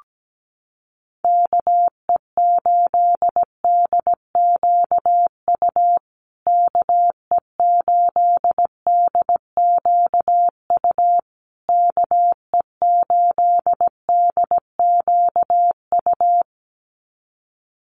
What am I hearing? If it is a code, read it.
Morse code, heard as a keyed tone, about 17 wpm, 700 Hz. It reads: KE8DQU KE8DQU KE8DQU
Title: USA Call Signs Repeated 3x 17wpm